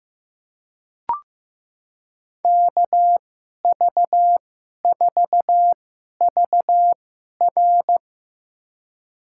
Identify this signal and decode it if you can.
Morse code, heard as a keyed tone, about 16 wpm, spaced as for 11 wpm, 700 Hz. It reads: KV4VR
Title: USA Call Signs - 5 Characters - 2x Character Spacing - 15wpm